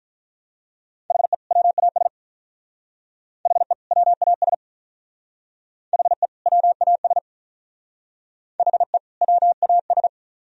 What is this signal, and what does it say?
Morse code, heard as a keyed tone, about 47 wpm, 700 Hz. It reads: HE WAS HE WAS HE WAS HE WAS